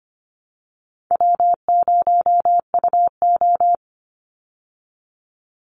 Morse code, heard as a keyed tone, about 25 wpm, 700 Hz. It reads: W0UO
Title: Easy Call Signs Repeated 2x - 25wpm